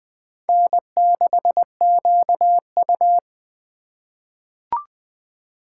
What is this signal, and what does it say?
Morse code, heard as a keyed tone, about 20 wpm, 700 Hz. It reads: N6QU
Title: USA Call Signs - 4 Characters - 6x Character Spacing - 20wpm